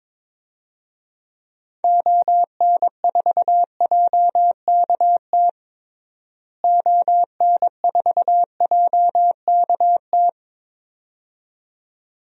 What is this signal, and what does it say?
Morse code, heard as a keyed tone, about 22 wpm, 700 Hz. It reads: ON4JKT ON4JKT
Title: DXCC-100 Call Signs - 6 Characters - Repeated 2x - 22wpm